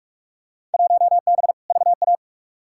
Morse code, heard as a keyed tone, about 46 wpm, 700 Hz. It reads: 1B VA
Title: Field Day Exchanges Encore - 45wpm